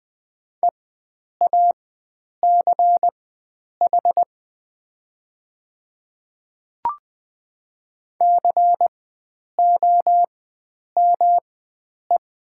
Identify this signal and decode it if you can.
Morse code, heard as a keyed tone, about 20 wpm, spaced as for 9 wpm, 700 Hz. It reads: EACH COME
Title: 4x Character Spacing - Top 100 Words - 20wpm